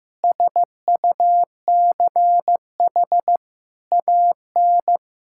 Morse code, heard as a keyed tone, about 16 wpm, 700 Hz. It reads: SUCH AN
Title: Sentences from Top 1000 Words - Review Part 5 - 15wpm